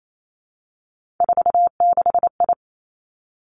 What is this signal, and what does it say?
Morse code, heard as a keyed tone, about 28 wpm, 700 Hz. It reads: 46I
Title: Three Challenging Letter-Number Combinations 28wpm